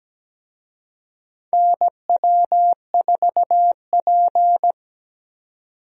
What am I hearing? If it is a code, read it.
Morse code, heard as a keyed tone, about 17 wpm, 700 Hz. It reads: NW4P
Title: USA Call Signs - 4 Characters - 17wpm